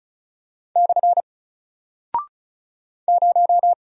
Morse code, heard as a keyed tone, about 36 wpm, 700 Hz. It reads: / 0
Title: Single Character (Letter:Number:Punctuation:Prosign) Alt 35wpm